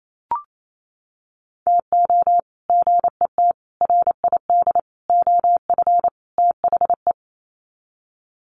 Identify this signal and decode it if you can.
Morse code, heard as a keyed tone, about 28 wpm, 700 Hz. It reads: TO GET RID OF THE
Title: Sets of 5 Words 28wpm